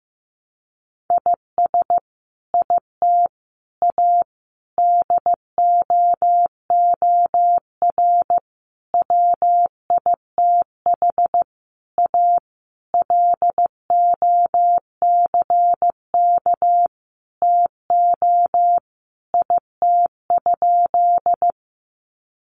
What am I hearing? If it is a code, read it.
Morse code, heard as a keyed tone, about 15 wpm, 700 Hz. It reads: IS IT A DOOR WITH A LOCK TO IT?
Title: Sentences from Top 1300 Words 15wpm